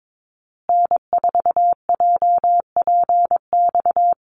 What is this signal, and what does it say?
Morse code, heard as a keyed tone, about 22 wpm, 700 Hz. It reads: N4JPX
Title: USA Call Signs Repeated 3x 22wpm